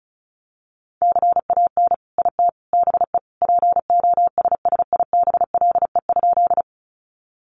Morse code, heard as a keyed tone, about 34 wpm, 700 Hz. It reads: CAN IT BE POSSIBLE?
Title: Sentences from Top 500 Words 35wpm